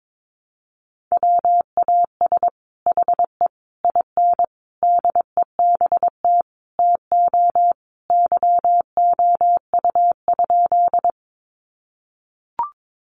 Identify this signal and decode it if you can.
Morse code, heard as a keyed tone, about 22 wpm, 700 Hz. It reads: WAS HE IN DEBT TO YOU?
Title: Sentences from Top 1300 Words 22wpm